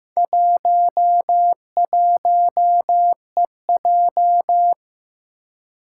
Morse code, heard as a keyed tone, about 15 wpm, 700 Hz. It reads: 11EJ
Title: Four Challenging Letter-Number Combinations 15wpm